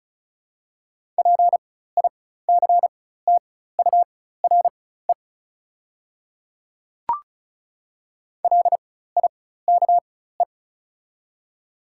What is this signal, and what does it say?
Morse code, heard as a keyed tone, about 36 wpm, spaced as for 16 wpm, 700 Hz. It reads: PICTURE LIKE
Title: 4x Character Spacing - Top 200 Words - 35wpm